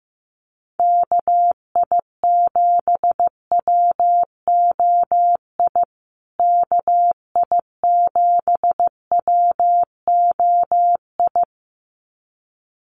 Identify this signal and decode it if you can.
Morse code, heard as a keyed tone, about 15 wpm, 700 Hz. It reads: KI7WOI KI7WOI
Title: USA Call Signs Repeated 2x 15wpm